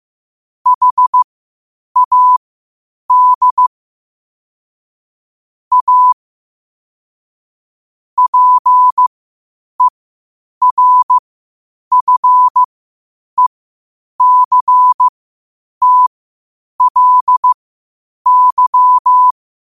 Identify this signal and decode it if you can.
Morse code, heard as a keyed tone, about 15 wpm, spaced as for 8 wpm, 1000 Hz. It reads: HAD A PERFECTLY
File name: qotdfiles / 8wpm